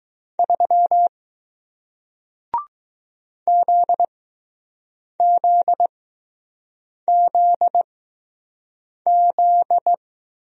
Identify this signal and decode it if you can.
Morse code, heard as a keyed tone, about 20 wpm, spaced as for 6 wpm, 700 Hz. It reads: 3 ZZZZ